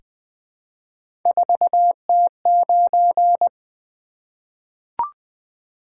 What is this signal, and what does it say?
Morse code, heard as a keyed tone, about 20 wpm, 700 Hz. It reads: 4T9
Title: Three Letter-Number Combinations - 20wpm